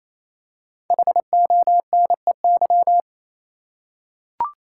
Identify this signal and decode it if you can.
Morse code, heard as a keyed tone, about 28 wpm, 700 Hz. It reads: HONEY
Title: Five Letter Words - 28wpm